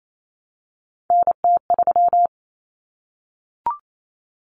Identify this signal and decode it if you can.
Morse code, heard as a keyed tone, about 28 wpm, 700 Hz. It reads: NT3
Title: Common Cut Numbers - 3-Digit 28wpm